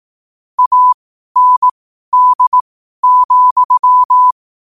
Morse code, heard as a keyed tone, about 18 wpm, spaced as for 12 wpm, 1000 Hz. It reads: AND,
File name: qotdfiles / 12wpm